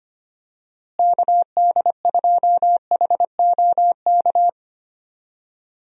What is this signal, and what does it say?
Morse code, heard as a keyed tone, about 25 wpm, 700 Hz. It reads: KD2HOK
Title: USA Call Signs - 6 Characters - Repeated 2x - 25wpm